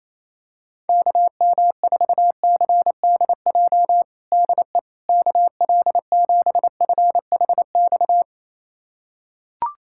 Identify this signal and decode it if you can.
Morse code, heard as a keyed tone, about 28 wpm, 700 Hz. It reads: KM4CDJ DE KL7FHX
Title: USA Callsign DE Callsign - 6 Characters - 28wpm